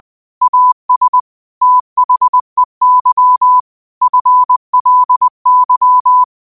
Morse code, heard as a keyed tone, about 20 wpm, 1000 Hz. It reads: AS THEY FLY